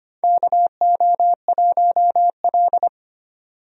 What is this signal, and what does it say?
Morse code, heard as a keyed tone, about 25 wpm, 700 Hz. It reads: KO1L
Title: USA Call Signs - 4 Characters - 25wpm